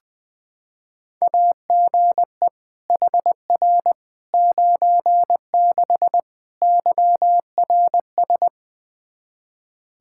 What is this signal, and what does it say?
Morse code, heard as a keyed tone, about 20 wpm, 700 Hz. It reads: AGE HR 96 YRS
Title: Practice QSO - Exchange 3 - More Info - 20wpm